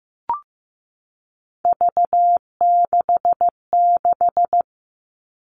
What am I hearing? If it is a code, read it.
Morse code, heard as a keyed tone, about 15 wpm, 700 Hz. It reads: V66
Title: All Cut Numbers - 3-Digit 15wpm